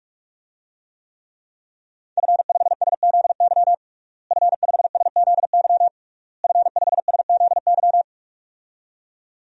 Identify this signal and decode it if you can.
Morse code, heard as a keyed tone, about 47 wpm, 700 Hz. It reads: F5SZY F5SZY F5SZY
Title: DXCC-100 Call Signs - 5 Characters - Repeated 3x - 45wpm